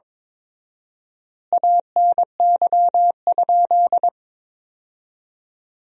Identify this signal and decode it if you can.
Morse code, heard as a keyed tone, about 22 wpm, 700 Hz. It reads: ANY?
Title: 4x Word-Spacing - Sentences from Top 300 Words - 22wpm